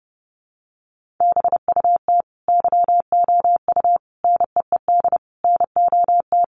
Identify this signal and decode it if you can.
Morse code, heard as a keyed tone, about 30 wpm, 700 Hz. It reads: BUT YOU NEED NOT
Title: Sentences from Top 1300 Words 30wpm